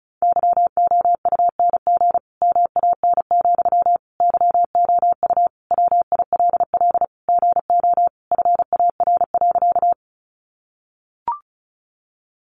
Morse code, heard as a keyed tone, about 34 wpm, 700 Hz. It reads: YOUNG MAN, YOU WILL GO FAR.
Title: Sentences from Top 400 Words 35wpm